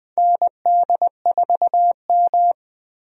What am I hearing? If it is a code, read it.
Morse code, heard as a keyed tone, about 20 wpm, 700 Hz. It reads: ND4M
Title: USA Call Signs - 4 Characters - 20wpm